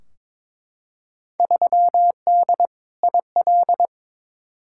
Morse code, heard as a keyed tone, about 22 wpm, 700 Hz. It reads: 3D IL
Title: Field Day Exchanges Encore - 22wpm